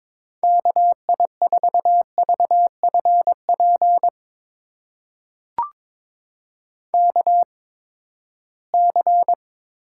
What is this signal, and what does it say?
Morse code, heard as a keyed tone, about 22 wpm, spaced as for 15 wpm, 700 Hz. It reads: KI4VFP KC
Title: USA Call Signs - 6 Characters - 8x Character Spacing - 22wpm